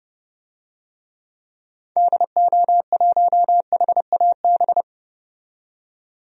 Morse code, heard as a keyed tone, about 30 wpm, 700 Hz. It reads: DO1HAB
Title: DXCC-100 Call Signs - 6 Characters - 30wpm